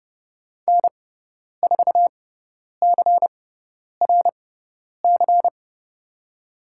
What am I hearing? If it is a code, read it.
Morse code, heard as a keyed tone, about 30 wpm, spaced as for 10 wpm, 690 Hz. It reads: N4CRC
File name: Lesson 17g - Letter u - Call Signs 30wpm